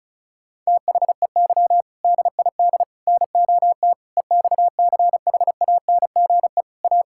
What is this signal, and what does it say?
Morse code, heard as a keyed tone, about 36 wpm, 700 Hz. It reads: THEY DID NOT EXCHANGE A
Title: Sentences from Top 1000 Words - Review Part 6 - 35wpm